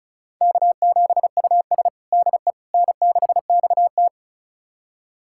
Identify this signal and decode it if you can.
Morse code, heard as a keyed tone, about 36 wpm, 700 Hz. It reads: K7US DE N6XT
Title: USA Callsign DE Callsign - 4 Characters - 35wpm